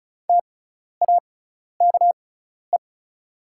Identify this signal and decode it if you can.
Morse code, heard as a keyed tone, about 34 wpm, spaced as for 12 wpm, 690 Hz. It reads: TAKE
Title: 6x Character Spacing - Top 500 Words - 35wpm